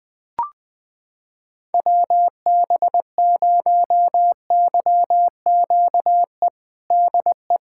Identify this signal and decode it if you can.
Morse code, heard as a keyed tone, about 20 wpm, 700 Hz. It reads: WB0YQE DE
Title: USA Callsign DE Callsign - 6 Characters - 20wpm